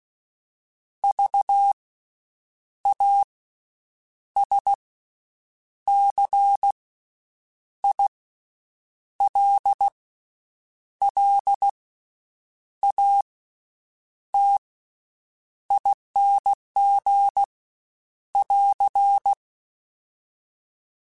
Morse code, heard as a keyed tone, about 16 wpm, spaced as for 7 wpm, 800 Hz. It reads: VASCILLATING <AR>
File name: Special INGs / Special INGs-mp3s / VerySlowToFaster Ing @15